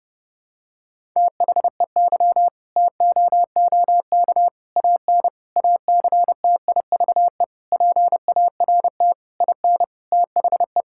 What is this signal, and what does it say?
Morse code, heard as a keyed tone, about 30 wpm, 700 Hz. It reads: THEY TOOK AN ACTIVE PART IN THE
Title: Sentences from Top 1000 Words - Review Part 6 - 30wpm